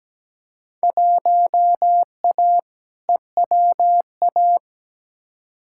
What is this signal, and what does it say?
Morse code, heard as a keyed tone, about 17 wpm, 700 Hz. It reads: 1A EWA
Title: Field Day Exchanges Encore - 17wpm